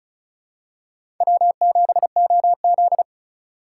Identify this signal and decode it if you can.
Morse code, heard as a keyed tone, about 36 wpm, 700 Hz. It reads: W7OZ
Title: USA Call Signs - 4 Characters - 4x Character Spacing - 35wpm